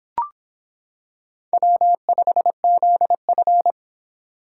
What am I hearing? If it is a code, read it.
Morse code, heard as a keyed tone, about 25 wpm, 700 Hz. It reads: W5ZF